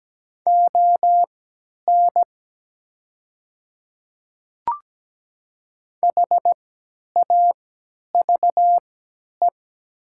Morse code, heard as a keyed tone, about 17 wpm, spaced as for 10 wpm, 700 Hz. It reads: ON HAVE
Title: 3x Character Spacing - Top 300 Words - 17wpm